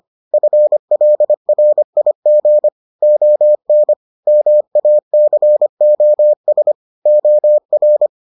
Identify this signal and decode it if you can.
Morse code, heard as a keyed tone, about 25 wpm, 600 Hz. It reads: FLRIG ON MACOS OR